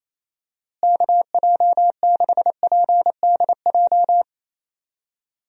Morse code, heard as a keyed tone, about 27 wpm, 700 Hz. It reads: KJ6PDJ
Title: USA Call Signs - 6 Characters - Repeated 2x - 28wpm